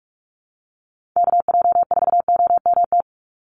Morse code, heard as a keyed tone, about 44 wpm, 700 Hz. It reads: KJ4OMT
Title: USA Call Signs 45wpm